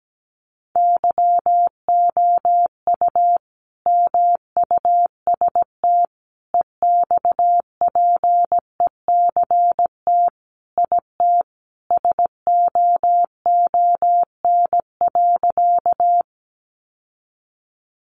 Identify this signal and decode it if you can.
Morse code, heard as a keyed tone, about 17 wpm, 700 Hz. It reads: YOU MUST EXPECT IT SOON.
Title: Sentences from Top 1100 Words 17wpm